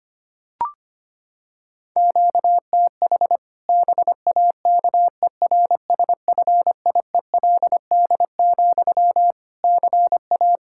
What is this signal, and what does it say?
Morse code, heard as a keyed tone, about 25 wpm, 700 Hz. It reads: QTH BAKERSFIELD, CA
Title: Practice QSO - Exchange 1 - Beginning - 25wpm